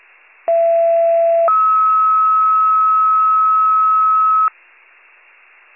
Two-tone page, heard as a one-second tone, then a three-second tone, about 670 then 1290 Hz. It is heard from 0.5 to 4.5 s.